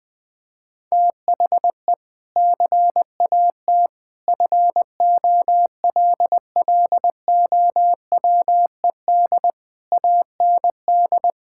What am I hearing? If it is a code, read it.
Morse code, heard as a keyed tone, about 20 wpm, 700 Hz. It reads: THE CAT FOLLOWED AND
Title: Sentences with 2-Word Phrases - 20wpm